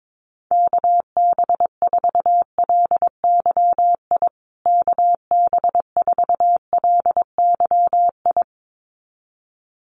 Morse code, heard as a keyed tone, about 22 wpm, 700 Hz. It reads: KB4LYI KB4LYI
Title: USA Call Signs Repeated 2x 22wpm